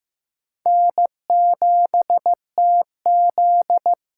Morse code, heard as a keyed tone, about 15 wpm, 700 Hz. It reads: N7TZ
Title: USA Call Signs - 5 Characters - 6x Character Spacing - 15wpm